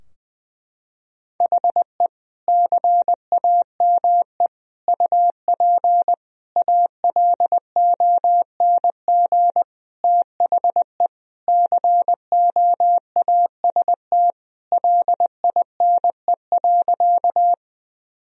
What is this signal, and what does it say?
Morse code, heard as a keyed tone, about 20 wpm, 700 Hz. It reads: HE CAME UP ALONG THE COAST LINE.
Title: Sentences from Top 1000 Words - Review Part 4 - 20wpm